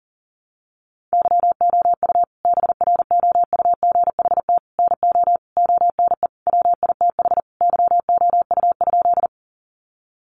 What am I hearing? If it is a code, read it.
Morse code, heard as a keyed tone, about 40 wpm, 700 Hz. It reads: YOU BROUGHT NO ONE WITH YOU?
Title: Sentences from Top 500 Words 40wpm